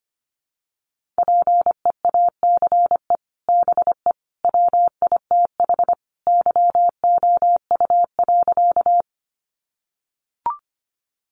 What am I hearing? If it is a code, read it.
Morse code, heard as a keyed tone, about 25 wpm, 700 Hz. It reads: PEACE BE WITH YOU.
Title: Sentences from Top 1500 Words 25wpm